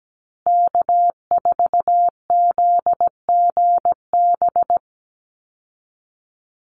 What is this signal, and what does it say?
Morse code, heard as a keyed tone, about 17 wpm, 700 Hz. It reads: K4ZGB
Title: Call Signs Repeated 2x - 17wpm